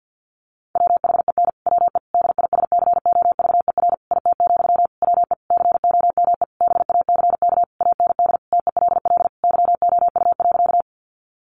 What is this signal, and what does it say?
Morse code, heard as a keyed tone, about 48 wpm, 700 Hz. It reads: WHEN WE DISCOVER IT, WE COME BACK AND TELL YOU.
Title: Sentences from Top 1100 Words 50wpm